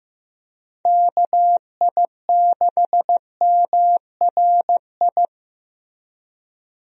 Morse code, heard as a keyed tone, about 15 wpm, 700 Hz. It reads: KI6MRI
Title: USA Call Signs - 6 Characters - Repeated 3x - 15wpm